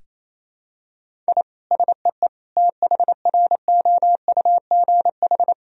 Morse code, heard as a keyed tone, about 28 wpm, 700 Hz. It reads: I SEE THROUGH
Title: Sentences from Top 1000 Words - Review Part 1 - 28wpm